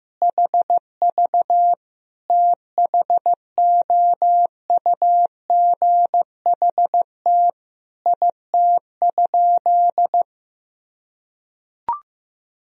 Morse code, heard as a keyed tone, about 16 wpm, 700 Hz. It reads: HV THOUGHT IT?
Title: Sentences from Top 300 Words and Abbreviations 15wpm